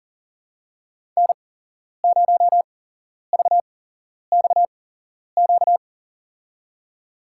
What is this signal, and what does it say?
Morse code, heard as a keyed tone, about 40 wpm, spaced as for 11 wpm, 700 Hz. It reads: N0VXQ
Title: USA Call Signs - 5 Characters - 8x Character Spacing - 40wpm